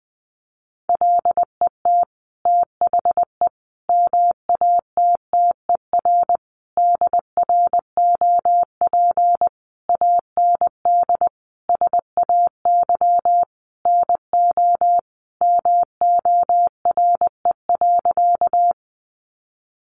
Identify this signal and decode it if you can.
Morse code, heard as a keyed tone, about 20 wpm, 700 Hz. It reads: LET THE MATTER DROP AND SAY NO MORE.
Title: Sentences from Top 1400 Words 20wpm